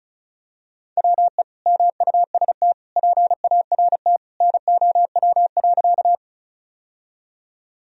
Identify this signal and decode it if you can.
Morse code, heard as a keyed tone, about 36 wpm, 700 Hz. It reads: WE MUST PART NOW.
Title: Sentences from Top 1000 Words - Review Part 1 - 35wpm